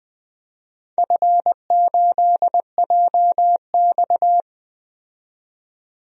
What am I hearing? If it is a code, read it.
Morse code, heard as a keyed tone, about 20 wpm, 700 Hz. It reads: F8JX
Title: DXCC-100 Call Signs - 4 Characters - 20wpm